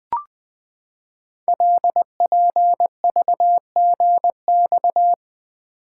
Morse code, heard as a keyed tone, about 20 wpm, 700 Hz. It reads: LPVGX
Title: Code Groups - 5 Letters - 20wpm